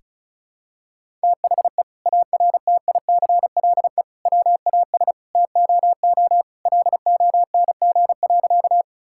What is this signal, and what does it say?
Morse code, heard as a keyed tone, about 34 wpm, 700 Hz. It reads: THE ARTICLE WAS TOO LONG.
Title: Sentences from Top 1000 Words - Review Part 5 - 35wpm